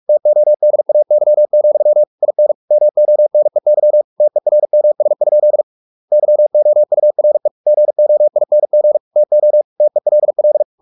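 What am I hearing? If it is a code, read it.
Morse code, heard as a keyed tone, about 44 wpm, 600 Hz. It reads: TODAY, IN MONEY TERMS? YOURE GOING TO TELL
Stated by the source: Morse practice at 45 wpm, the novel Never by Ken Follett in English, from CW med Gnister.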